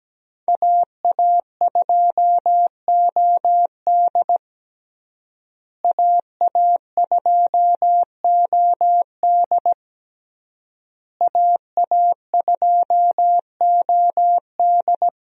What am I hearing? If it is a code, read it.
Morse code, heard as a keyed tone, about 17 wpm, 700 Hz. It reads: AA2OD AA2OD AA2OD
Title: USA Call Signs - 5 Characters - Repeated 3x - 17wpm